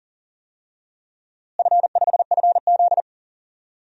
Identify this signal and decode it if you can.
Morse code, heard as a keyed tone, about 40 wpm, 700 Hz. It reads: F5FZ
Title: DXCC-100 Call Signs - 4 Characters - Repeated 2x - 40wpm